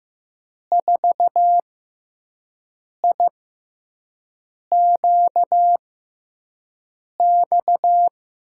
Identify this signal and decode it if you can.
Morse code, heard as a keyed tone, about 15 wpm, 700 Hz. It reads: 4 I Q X
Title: USA Call Signs - 5 Characters - 6x Character Spacing - 15wpm